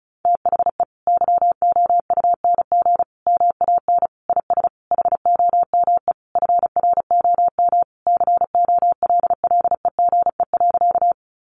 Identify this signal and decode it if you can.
Morse code, heard as a keyed tone, about 34 wpm, 700 Hz. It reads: THE YOUNG MAN IS HOME FROM COLLEGE.
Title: Sentences from Top 1300 Words 35wpm